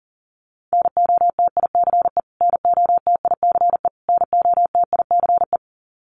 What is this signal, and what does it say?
Morse code, heard as a keyed tone, about 40 wpm, 700 Hz. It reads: NOTICE NOTICE NOTICE
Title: Top 500 Words Repeated 40wpm